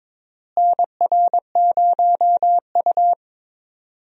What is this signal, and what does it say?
Morse code, heard as a keyed tone, about 22 wpm, 700 Hz. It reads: NR0U
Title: USA Call Signs - 4 Characters - Repeated 3x - 22wpm